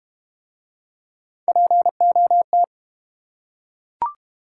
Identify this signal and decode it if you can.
Morse code, heard as a keyed tone, about 30 wpm, 700 Hz. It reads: POT
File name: Easy Words - Speed-Racing with Hints - 25wpm